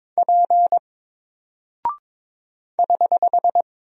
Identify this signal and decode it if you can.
Morse code, heard as a keyed tone, about 22 wpm, 700 Hz. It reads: P <HH>
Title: Single Character (Letter:Number:Punctuation:Prosign) 22wpm